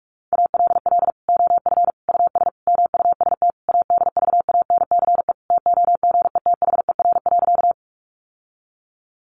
Morse code, heard as a keyed tone, about 44 wpm, 700 Hz. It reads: ALL OF US MUST ADVANCE TOGETHER.
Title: Sentences from Top 1400 Words 45wpm